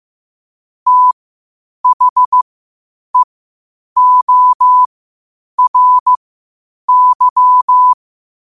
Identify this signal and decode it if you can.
Morse code, heard as a keyed tone, about 15 wpm, spaced as for 8 wpm, 1000 Hz. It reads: THEORY
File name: qotdfiles / 8wpm